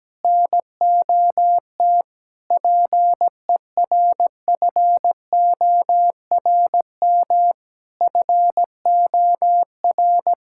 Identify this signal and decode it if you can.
Morse code, heard as a keyed tone, about 17 wpm, 700 Hz. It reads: NOT PERFORM FOR
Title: Sentences from Top 1000 Words - Review Part 5 - 17wpm